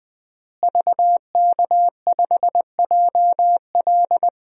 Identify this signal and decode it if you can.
Morse code, heard as a keyed tone, about 21 wpm, 700 Hz. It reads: VK5JL
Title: DXCC-100 Call Signs - 5 Characters - Repeated 3x - 20wpm